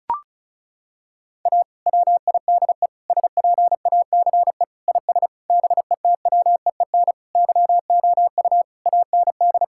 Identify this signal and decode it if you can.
Morse code, heard as a keyed tone, about 36 wpm, 700 Hz. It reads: A WIDE SPACE IS BETWEEN YOU AND
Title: Sentences from Top 1000 Words - Review Part 6 - 35wpm